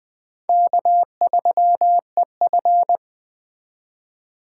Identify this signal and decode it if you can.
Morse code, heard as a keyed tone, about 20 wpm, 700 Hz. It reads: K3EF